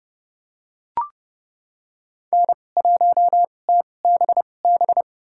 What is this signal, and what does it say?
Morse code, heard as a keyed tone, about 30 wpm, spaced as for 22 wpm, 700 Hz. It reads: N1TBB
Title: USA Call Signs - 5 Characters - 2x Character Spacing - 30wpm